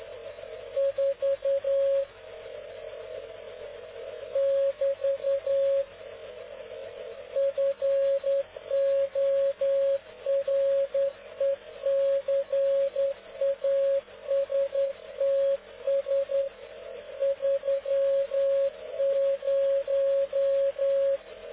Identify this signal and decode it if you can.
Morse code, heard as a keyed tone, about 10 wpm, 550 Hz. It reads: 4 <BT> FORECASTS 30
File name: cw / ubacw dk0wcy slow